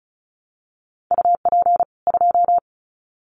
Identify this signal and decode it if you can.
Morse code, heard as a keyed tone, about 34 wpm, 700 Hz. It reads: UP 2
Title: Common QSO Phrases 35wpm